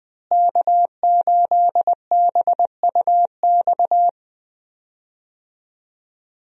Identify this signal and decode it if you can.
Morse code, heard as a keyed tone, about 20 wpm, 700 Hz. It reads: K8BUX
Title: USA Call Signs - 5 Characters - Repeated 3x - 20wpm